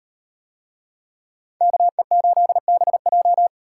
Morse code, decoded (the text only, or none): KE8BJ